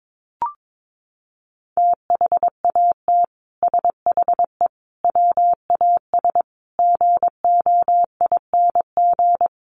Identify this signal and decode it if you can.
Morse code, heard as a keyed tone, about 22 wpm, 700 Hz. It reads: THAT SHE WAS GOING
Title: Sets of 5 Words Encore2 22wpm